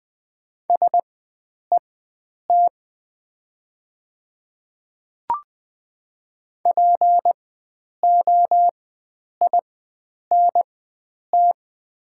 Morse code, heard as a keyed tone, about 20 wpm, spaced as for 9 wpm, 700 Hz. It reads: SET POINT
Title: 4x Character Spacing - Top 300 Words - 20wpm